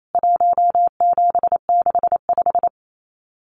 Morse code, heard as a keyed tone, about 28 wpm, 700 Hz. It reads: 1765
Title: CWT Exchange 28wpm